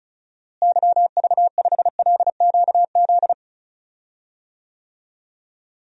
Morse code, heard as a keyed tone, about 36 wpm, 700 Hz. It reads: YV5LQZ
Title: DXCC-100 Call Signs - 6 Characters - Repeated 2x - 35wpm